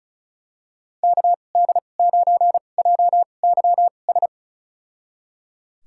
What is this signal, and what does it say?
Morse code, heard as a keyed tone, about 34 wpm, spaced as for 25 wpm, 690 Hz. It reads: KD9JYS
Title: USA Call Signs - 6 Characters - 2x Character Spacing - 35wpm